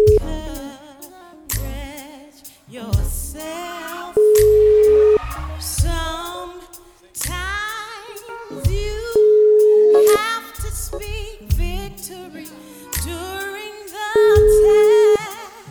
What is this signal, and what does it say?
Ringback tone (Europe, 425 Hz).